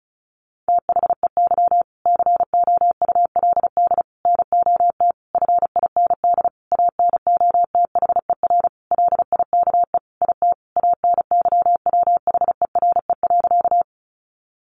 Morse code, heard as a keyed tone, about 34 wpm, 700 Hz. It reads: THEY COULD NOT FIND ANOTHER LIKE IT ANYWHERE.
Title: Sentences from Top 1400 Words 35wpm